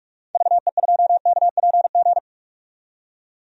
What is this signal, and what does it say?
Morse code, heard as a keyed tone, about 43 wpm, 700 Hz. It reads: VE2KPG
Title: DXCC-100 Call Signs - 6 Characters - 45wpm